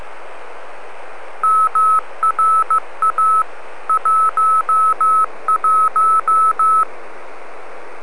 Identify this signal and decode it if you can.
Morse code, heard as a keyed tone, about 15 wpm, 1300 Hz. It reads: MRA 11